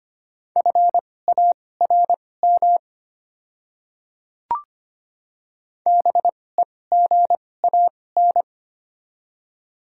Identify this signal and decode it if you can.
Morse code, heard as a keyed tone, about 25 wpm, spaced as for 18 wpm, 700 Hz. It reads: FARM BEGAN
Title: 2x Character Spacing - Top 500 Words - 25wpm